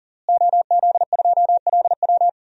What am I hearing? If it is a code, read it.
Morse code, heard as a keyed tone, about 37 wpm, 700 Hz. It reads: OZ2LW
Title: DXCC-100 Call Signs - 5 Characters - Repeated 3x - 40wpm